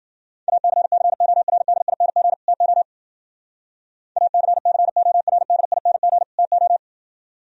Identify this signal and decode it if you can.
Morse code, heard as a keyed tone, about 48 wpm, 700 Hz. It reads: ACCORDING TO ACCORDING TO
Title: Speed-Racing - Sets of 2 Words 40wpm